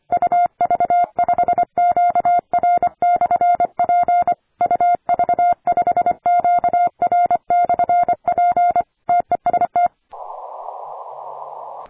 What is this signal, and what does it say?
Morse code, heard as a keyed tone, about 25 wpm, 700 Hz. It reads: UV5QR/P UV5QR/P TEST
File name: attachments / 40m UU7JF P